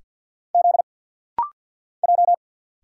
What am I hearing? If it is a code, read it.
Morse code, heard as a keyed tone, about 50 wpm, 690 Hz. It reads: Z J
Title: Single Letters - Rapid-Fire 50wpm